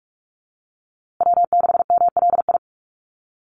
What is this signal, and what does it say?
Morse code, heard as a keyed tone, about 44 wpm, 700 Hz. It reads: W6MLI
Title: USA Call Signs Repeated 2x 45wpm